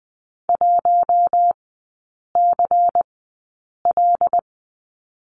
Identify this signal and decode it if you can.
Morse code, heard as a keyed tone, about 20 wpm, spaced as for 8 wpm, 700 Hz. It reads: 1CL
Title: 133 Lesson 22e - Question Mark - Three Characters 20wpm